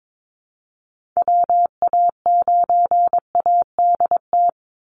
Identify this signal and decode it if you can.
Morse code, heard as a keyed tone, about 22 wpm, 700 Hz. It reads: WA9ADT
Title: Call Signs Repeated 3x - 22wpm